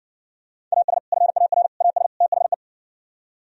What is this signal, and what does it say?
Morse code, heard as a keyed tone, about 59 wpm, 700 Hz. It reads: AS FAR AS THE